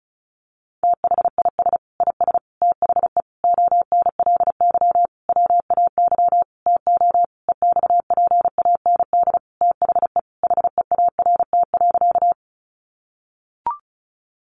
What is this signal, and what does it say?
Morse code, heard as a keyed tone, about 34 wpm, 700 Hz. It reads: THIS IS THE ONLY WAY TO EXPAND THE HEART.
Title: Sentences from Top 1400 Words 35wpm